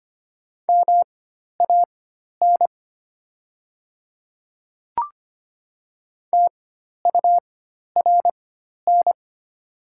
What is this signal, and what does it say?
Morse code, heard as a keyed tone, about 25 wpm, spaced as for 12 wpm, 700 Hz. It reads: MAN TURN
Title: 4x Character Spacing - Top 200 Words - 25wpm